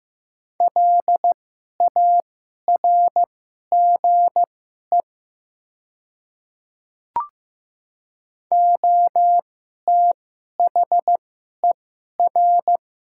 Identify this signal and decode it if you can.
Morse code, heard as a keyed tone, about 15 wpm, spaced as for 11 wpm, 700 Hz. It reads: LARGE OTHER
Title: 2x Character Spacing - Top 300 Words - 15wpm